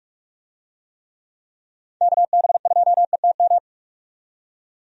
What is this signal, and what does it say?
Morse code, heard as a keyed tone, about 46 wpm, 700 Hz. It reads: KB2ETM
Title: USA Call Signs - 6 Characters - 45wpm